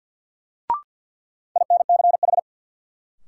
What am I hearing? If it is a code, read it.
Morse code, heard as a keyed tone, about 53 wpm, 700 Hz. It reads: INCH
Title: Sentences from Top 1000 Words - Review Part 3 - New Word Drill - 50wpm